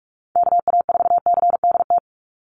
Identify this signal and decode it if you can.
Morse code, heard as a keyed tone, about 44 wpm, 700 Hz. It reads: KA4CDT
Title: USA Call Signs 45wpm